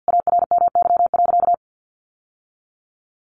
Morse code, heard as a keyed tone, about 50 wpm, 700 Hz. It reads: ARMY.
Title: Sentences from Top 1200 Words 50wpm